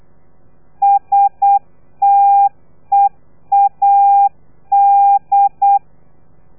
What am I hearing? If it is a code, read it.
Morse code, heard as a keyed tone, about 8 wpm, 800 Hz. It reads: STEAD
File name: telegrafia / i2tfj / Lez3 35